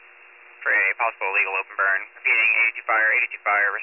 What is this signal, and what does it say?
Emergency Alert System (SAME) tones, heard as harsh buzzing bursts, with speech. It is heard from 0.6 to 3.3 s.